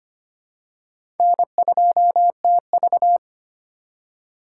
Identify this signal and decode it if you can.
Morse code, heard as a keyed tone, about 25 wpm, 700 Hz. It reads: N2TV
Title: USA Call Signs - 4 Characters - 3x Character Spacing - 25wpm